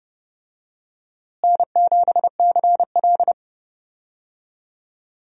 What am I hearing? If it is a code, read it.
Morse code, heard as a keyed tone, about 30 wpm, 700 Hz. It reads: N7CL
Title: USA Call Signs - 4 Characters - Repeated 3x - 30wpm